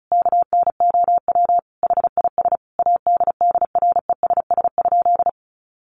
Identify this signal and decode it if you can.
Morse code, heard as a keyed tone, about 34 wpm, 700 Hz. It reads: KNOW HIS ADDRESS?
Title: Sentences from Top 1000 Words 35wpm